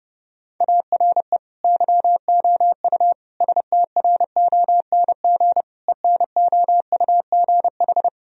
Morse code, heard as a keyed tone, about 30 wpm, 700 Hz. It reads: ARE YOU STRONG ENOUGH
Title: Sentences from Top 1000 Words - Review Part 3 - 30wpm